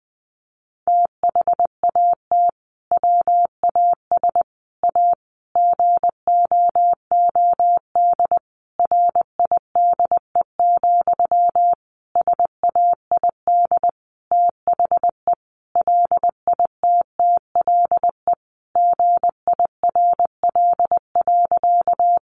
Morse code, heard as a keyed tone, about 20 wpm, 700 Hz. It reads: THAT WAS A GOOD RIDE, SAID THE LITTLE GIRL.
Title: Sentences from Top 1200 Words 20wpm